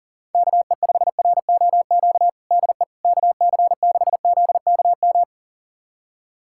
Morse code, heard as a keyed tone, about 40 wpm, 690 Hz. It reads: KE5ROQ DE KC6ZKM
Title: USA Callsign DE Callsign - 6 Characters - 40wpm